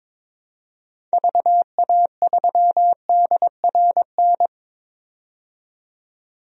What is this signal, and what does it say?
Morse code, heard as a keyed tone, about 22 wpm, 700 Hz. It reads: VA3DRN
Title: DXCC-100 Call Signs - 6 Characters - 22wpm